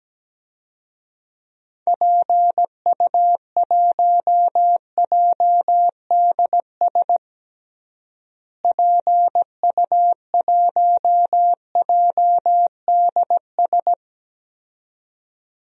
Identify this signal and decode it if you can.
Morse code, heard as a keyed tone, about 17 wpm, 700 Hz. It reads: PU1JDS PU1JDS
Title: DXCC-100 Call Signs - 6 Characters - Repeated 2x - 17wpm